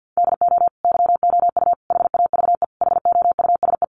Morse code, heard as a keyed tone, about 50 wpm, 700 Hz. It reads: DO YOU HAVE HOUSE
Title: Sentences from Top 1100 Words 50wpm